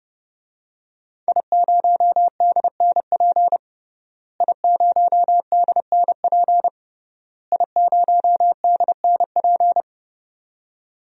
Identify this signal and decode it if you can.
Morse code, heard as a keyed tone, about 30 wpm, 700 Hz. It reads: I0DNP I0DNP I0DNP
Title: DXCC-100 Call Signs - 5 Characters - Repeated 3x - 30wpm